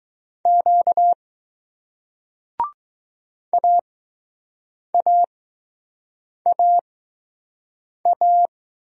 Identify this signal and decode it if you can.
Morse code, heard as a keyed tone, about 21 wpm, spaced as for 6 wpm, 700 Hz. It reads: Q AAAA